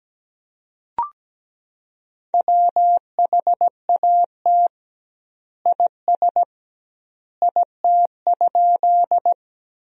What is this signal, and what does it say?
Morse code, heard as a keyed tone, about 17 wpm, 700 Hz. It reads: WHAT IS IT?
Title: 2x Word-Spacing - Sentences from Top 100 Words - 17wpm